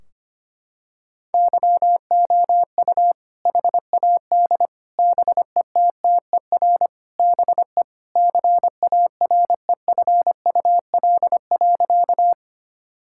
Morse code, heard as a keyed tone, about 25 wpm, 700 Hz. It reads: YOU HAD BETTER BE CAREFUL.
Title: Sentences from Top 1600 Words 25wpm